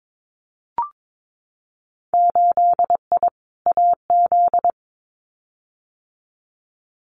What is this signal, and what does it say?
Morse code, heard as a keyed tone, about 22 wpm, 700 Hz. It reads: 8I AZ
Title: Winter Field Day Exchanges Encore 22wpm